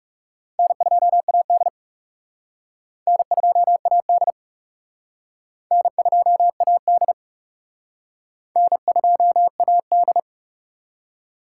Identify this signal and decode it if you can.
Morse code, heard as a keyed tone, about 36 wpm, 700 Hz. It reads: N2AD N2AD N2AD N2AD
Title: Speed Racing - USA 1x2 Call Signs 30wpm